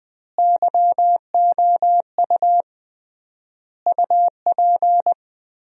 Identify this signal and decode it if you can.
Morse code, heard as a keyed tone, about 21 wpm, 700 Hz. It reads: YOU UP
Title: 3x Word-Spacing - Sentences from Top 100 Words - 20wpm